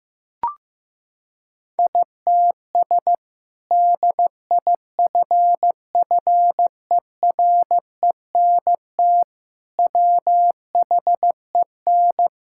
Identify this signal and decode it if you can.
Morse code, heard as a keyed tone, about 15 wpm, 700 Hz. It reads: ITS DIFFERENT WHEN
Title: Sentences from Top 1000 Words - Review Part 5 - 15wpm